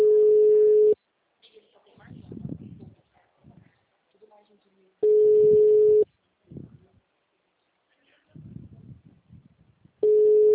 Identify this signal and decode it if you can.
Ringback tone (Europe, 425 Hz), with speech.
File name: out- -2010- - - .49